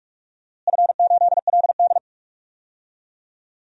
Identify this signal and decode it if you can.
Morse code, heard as a keyed tone, about 46 wpm, 700 Hz. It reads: F8LD
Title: DXCC-100 Call Signs - 4 Characters - Repeated 3x - 45wpm